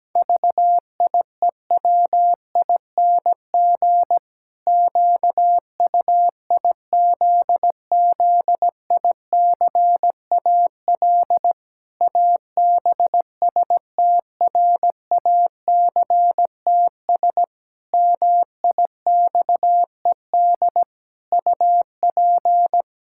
Morse code, heard as a keyed tone, about 17 wpm, 700 Hz. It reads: VIEWING QUIZZICAL ABSTRACTS MIXED UP
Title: Pangram Challenge v2 - 17wpm